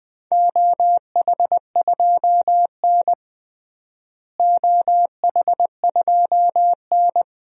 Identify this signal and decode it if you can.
Morse code, heard as a keyed tone, about 21 wpm, 700 Hz. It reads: OH2N OH2N
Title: DXCC-100 Call Signs - 4 Characters - Repeated 2x - 20wpm